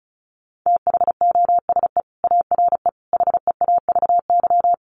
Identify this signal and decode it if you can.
Morse code, heard as a keyed tone, about 34 wpm, 700 Hz. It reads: THOSE ARE HEAVY
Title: Sentences from Top 1100 Words 35wpm